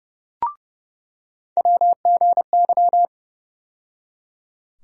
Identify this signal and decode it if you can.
Morse code, heard as a keyed tone, about 30 wpm, 700 Hz. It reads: WGY
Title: USA Call Sign Postfixes - 3 Letters 30wpm